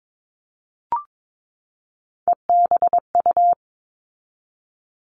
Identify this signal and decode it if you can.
Morse code, heard as a keyed tone, about 22 wpm, 700 Hz. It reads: EBU
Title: All Cut Numbers - 3-Digit 22wpm